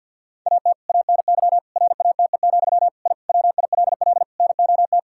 Morse code, heard as a keyed tone, about 50 wpm, 700 Hz. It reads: AT ANY RATE, I WILL NOT
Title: Sentences from Top 1000 Words - Review Part 6 - 50wpm